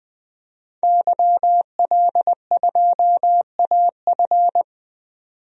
Morse code, heard as a keyed tone, about 21 wpm, 700 Hz. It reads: YL2AF
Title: DXCC-100 Call Signs - 5 Characters - 20wpm